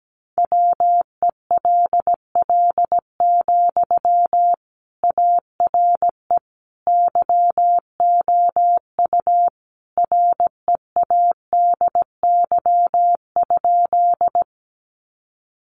Morse code, heard as a keyed tone, about 17 wpm, 700 Hz. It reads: WELL, ARE YOU READY?
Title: Sentences from Top 400 Words 17wpm